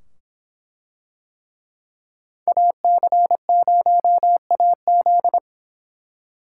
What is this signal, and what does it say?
Morse code, heard as a keyed tone, about 23 wpm, 700 Hz. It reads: AC0AZ